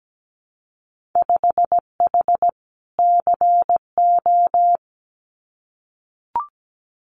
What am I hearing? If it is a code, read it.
Morse code, heard as a keyed tone, about 17 wpm, 700 Hz. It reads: 5H CO
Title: Winter Field Day Exchanges Encore 17wpm